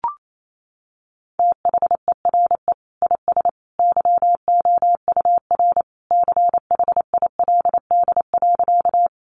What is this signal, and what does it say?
Morse code, heard as a keyed tone, about 28 wpm, 700 Hz. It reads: THERE IS YOUR CHILD.
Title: Sentences from Top 800 Words 28wpm